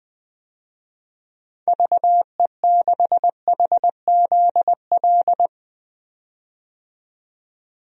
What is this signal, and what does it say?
Morse code, heard as a keyed tone, about 20 wpm, 700 Hz. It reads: VE6HZL
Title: DXCC-100 Call Signs - 6 Characters - 20wpm